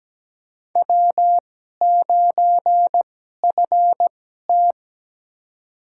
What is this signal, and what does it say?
Morse code, heard as a keyed tone, about 17 wpm, spaced as for 12 wpm, 700 Hz. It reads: W9FT
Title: USA Call Signs - 4 Characters - 2x Character Spacing - 17wpm